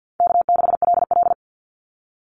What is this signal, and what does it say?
Morse code, heard as a keyed tone, about 50 wpm, 700 Hz. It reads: K6LL
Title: Call Signs - 50wpm